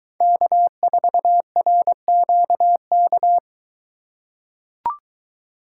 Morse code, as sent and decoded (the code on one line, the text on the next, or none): -.- ....- .-. --.- -.-
K4RQK